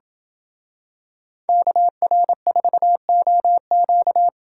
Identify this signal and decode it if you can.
Morse code, heard as a keyed tone, about 26 wpm, 700 Hz. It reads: KR4OQ